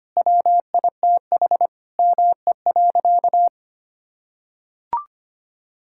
Morse code, heard as a keyed tone, about 25 wpm, 700 Hz. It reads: WITH ME.
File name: practice / 25wpm / Sentences with 9-Word Phrases - 25wpm